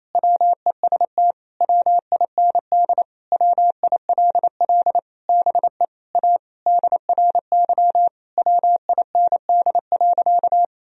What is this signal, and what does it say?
Morse code, heard as a keyed tone, about 28 wpm, 700 Hz. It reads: WEST WIND WILL BE A DRY WIND.
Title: Sentences from Top 1000 Words - Review Part 3 - 28wpm